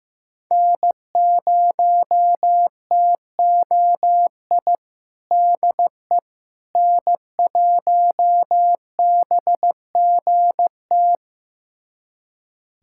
Morse code, heard as a keyed tone, about 15 wpm, 700 Hz. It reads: N0TOI DE N1BGT
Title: USA Callsign DE Callsign - 5 Characters - 15wpm